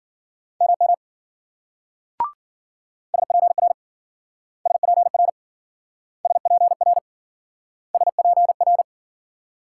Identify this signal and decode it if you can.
Morse code, heard as a keyed tone, about 52 wpm, 700 Hz. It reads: MM SPR SPR SPR SPR